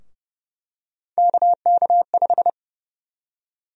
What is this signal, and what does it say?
Morse code, heard as a keyed tone, about 30 wpm, 700 Hz. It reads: KK5
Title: USA Call Sign Prefixes - Repeated 2x - 30wpm